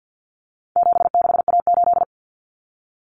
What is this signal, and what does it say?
Morse code, heard as a keyed tone, about 48 wpm, 700 Hz. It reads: 76A8
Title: Common Cut Numbers - 4-Digit 50wpm